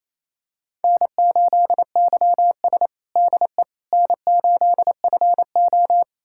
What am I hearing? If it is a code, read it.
Morse code, heard as a keyed tone, about 29 wpm, 690 Hz. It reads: N8YS DE N8FO
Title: USA Callsign DE Callsign - 4 Characters - 28wpm